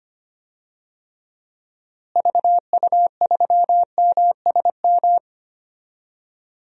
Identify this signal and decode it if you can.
Morse code, heard as a keyed tone, about 25 wpm, 700 Hz. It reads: VU3MSM
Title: DXCC-100 Call Signs - 6 Characters - Repeated 3x - 25wpm